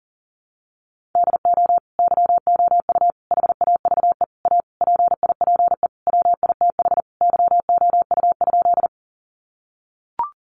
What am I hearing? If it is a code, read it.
Morse code, heard as a keyed tone, about 40 wpm, 700 Hz. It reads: DO YOU HAVE A PIPE WITH YOU?
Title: Sentences from Top 1500 Words 40wpm